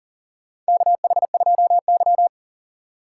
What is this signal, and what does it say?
Morse code, heard as a keyed tone, about 40 wpm, 700 Hz. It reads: KH2Y
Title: USA Call Signs - 4 Characters - 6x Character Spacing - 40wpm